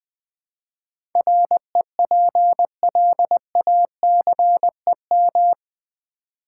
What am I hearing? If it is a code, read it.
Morse code, heard as a keyed tone, about 20 wpm, 700 Hz. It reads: REPLACEM
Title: Build A Word - 4 Letters At A Time - 11-Letter Words - 20wpm